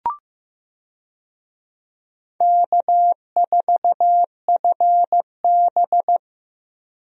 Morse code, heard as a keyed tone, about 15 wpm, 700 Hz. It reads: K4FB
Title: USA Call Signs - 4 Characters - Repeated 3x - 15wpm